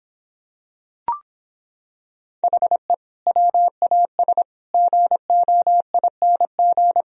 Morse code, heard as a keyed tone, about 26 wpm, 700 Hz. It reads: HE WAS GOING